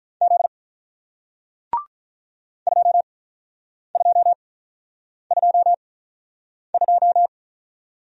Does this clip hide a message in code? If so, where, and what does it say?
Yes: Morse code from 0.2 to 7.3 s, heard as a keyed tone, about 48 wpm, spaced as for 9 wpm, 700 Hz. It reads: Z 2222